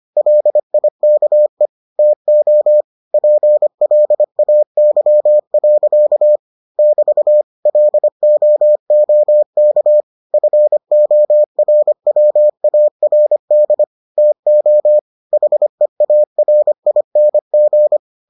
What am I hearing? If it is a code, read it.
Morse code, heard as a keyed tone, about 25 wpm, 600 Hz. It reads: LIKE TO PLAY. <BT> LOOK FORWARD TO HEARING